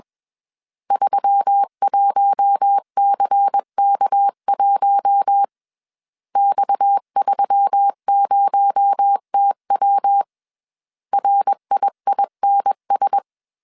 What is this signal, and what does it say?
Morse code, heard as a keyed tone, about 21 wpm, 800 Hz. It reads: 31CK1 X30TW RIINS
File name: CW / DET / 1 6 1500C S C 20wpm 2116